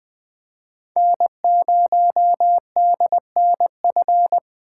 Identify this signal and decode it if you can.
Morse code, heard as a keyed tone, about 21 wpm, 700 Hz. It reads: N0DNF
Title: 2023 SOTA Chaser Call Signs - 20wpm